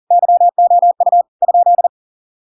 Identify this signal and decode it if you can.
Morse code, heard as a keyed tone, about 40 wpm, 700 Hz. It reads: YOU ?